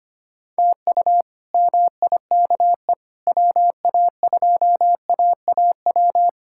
Morse code, heard as a keyed tone, about 25 wpm, 700 Hz. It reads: TU MIKE WA2AAW
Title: K1USN SST QSO Practice - 25wpm